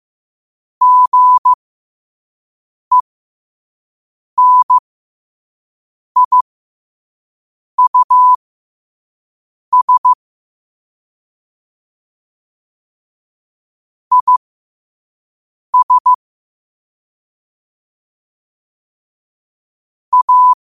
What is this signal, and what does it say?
Morse code, heard as a keyed tone, about 15 wpm, spaced as for 5 wpm, 1000 Hz. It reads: GENIUS IS A